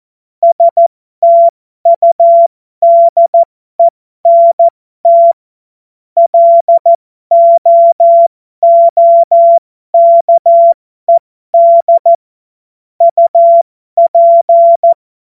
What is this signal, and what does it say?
Morse code, heard as a keyed tone, about 14 wpm, spaced as for 12 wpm, 680 Hz. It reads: STUDENT LOOKED UP
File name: Morsekurs / Witze